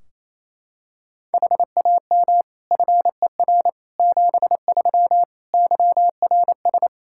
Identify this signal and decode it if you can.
Morse code, heard as a keyed tone, about 28 wpm, 700 Hz. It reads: HAM FER 73 YRS
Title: Practice QSO - Exchange 3 - More Info - 28wpm